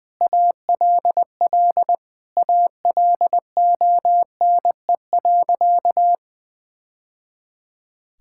Morse code, heard as a keyed tone, about 20 wpm, 700 Hz. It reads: ALL ALONE.
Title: Sentences from Top 1000 Words - Review Part 5 - 20wpm